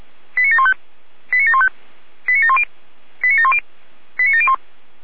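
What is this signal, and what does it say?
Five-tone selective call (ZVEI), heard as quick runs of five tones, tones of about 70 ms. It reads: 89715 89715 89710 89710 89701
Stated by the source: ZVEI five-tone call 89490 received by SDR, channel 455.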